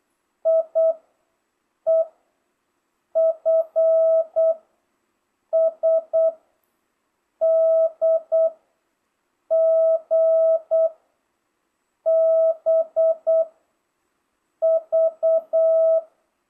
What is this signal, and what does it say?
Morse code, heard as a keyed tone, about 8 wpm, 650 Hz. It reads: I E F S D G B V